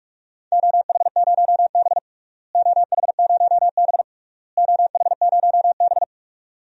Morse code, heard as a keyed tone, about 47 wpm, 700 Hz. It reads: OH0B OH0B OH0B
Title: DXCC-100 Call Signs - 4 Characters - Repeated 3x - 45wpm